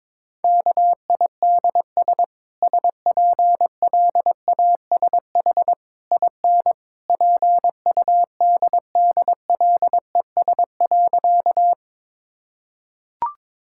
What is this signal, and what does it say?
Morse code, heard as a keyed tone, about 23 wpm, 700 Hz. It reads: KIDS SPLASH IN PUDDLES.
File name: practice / 22wpm / Easy Sentences with Hints - 22wpm